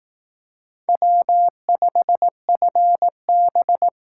Morse code, heard as a keyed tone, about 18 wpm, 700 Hz. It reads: W5FB
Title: Speed Racing - USA 1x2 Call Signs 15wpm